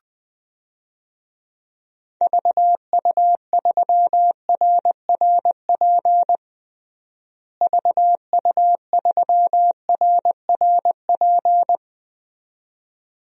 Morse code, heard as a keyed tone, about 20 wpm, 700 Hz. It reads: VU3RRP VU3RRP
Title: DXCC-100 Call Signs - 6 Characters - Repeated 2x - 20wpm